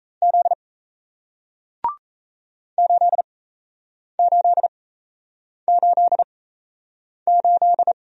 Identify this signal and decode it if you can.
Morse code, heard as a keyed tone, about 37 wpm, spaced as for 8 wpm, 700 Hz. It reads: Z 8888